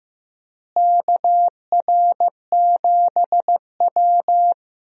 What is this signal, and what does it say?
Morse code, heard as a keyed tone, about 15 wpm, 700 Hz. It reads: KR7W
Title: USA Call Signs - 4 Characters - Repeated 3x - 15wpm